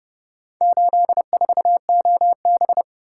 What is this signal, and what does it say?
Morse code, heard as a keyed tone, about 29 wpm, 700 Hz. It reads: 84OB